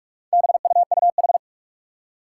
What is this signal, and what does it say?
Morse code, heard as a keyed tone, about 46 wpm, 700 Hz. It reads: BUUH
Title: Four Challenging Letter-Number Combinations 45wpm